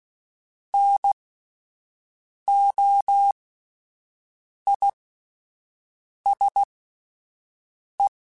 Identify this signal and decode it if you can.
Morse code, heard as a keyed tone, about 16 wpm, spaced as for 5 wpm, 800 Hz. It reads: NOISE
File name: Alphabetical / Alphabetical mp3s / Ns @5